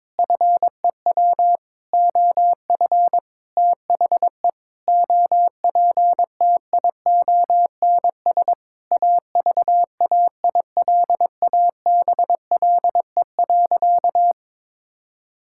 Morse code, heard as a keyed tone, about 22 wpm, 700 Hz. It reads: FEW OF THE OPTIONS AVAILABLE.
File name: Sentences with 7-Word Phrases - 22wpm